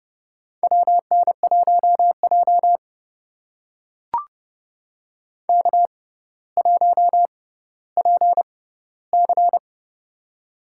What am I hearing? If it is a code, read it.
Morse code, heard as a keyed tone, about 30 wpm, spaced as for 15 wpm, 700 Hz. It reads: WN1J K1PC
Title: USA Call Signs - 6x Character Spacing - 30wpm